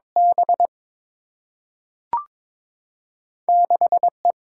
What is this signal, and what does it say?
Morse code, heard as a keyed tone, about 22 wpm, 700 Hz. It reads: B 6 E
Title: Single-Letter Single-Number 22wpm